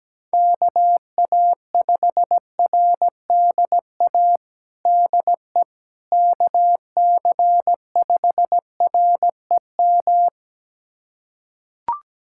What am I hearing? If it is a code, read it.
Morse code, heard as a keyed tone, about 17 wpm, 700 Hz. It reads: KA5RDA DE KC5REM
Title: USA Callsign DE Callsign 17wpm